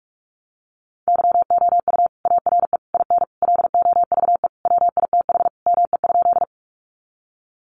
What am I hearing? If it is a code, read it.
Morse code, heard as a keyed tone, about 44 wpm, 700 Hz. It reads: YOU ARE IN LOVE WITH ME?
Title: Sentences from Top 400 Words 45wpm